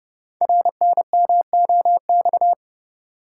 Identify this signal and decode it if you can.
Morse code, heard as a keyed tone, about 30 wpm, 700 Hz. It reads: RNMOX
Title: Code Groups - 5 Letters - 30wpm